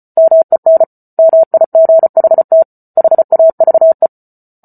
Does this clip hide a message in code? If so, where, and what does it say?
Yes: Morse code from 0.2 to 4.1 s, heard as a keyed tone, about 34 wpm, 650 Hz. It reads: MEN MIGHT HAVE